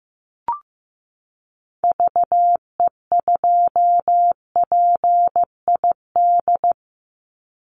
Morse code, heard as a keyed tone, about 15 wpm, 700 Hz. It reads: VE2PID
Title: Call Signs Repeated 3x - 15wpm